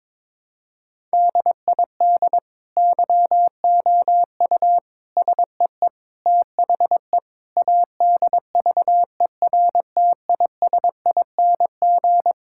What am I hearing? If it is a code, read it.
Morse code, heard as a keyed tone, about 22 wpm, 700 Hz. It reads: DID YOU SEE THE ADVERTISING
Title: Sentences from Top 1000 Words - Review Part 6 - 22wpm